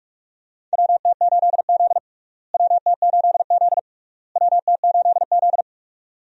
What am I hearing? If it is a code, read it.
Morse code, heard as a keyed tone, about 46 wpm, 700 Hz. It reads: WT8Z WT8Z WT8Z
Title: USA Call Signs - 4 Characters - Repeated 3x - 45wpm